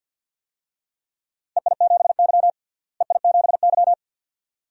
Morse code, heard as a keyed tone, about 49 wpm, 700 Hz. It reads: EI7Y EI7Y
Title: DXCC-100 Call Signs - 4 Characters - Repeated 2x - 50wpm